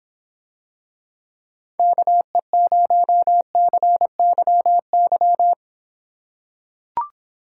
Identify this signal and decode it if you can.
Morse code, heard as a keyed tone, about 24 wpm, 700 Hz. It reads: KE0CYY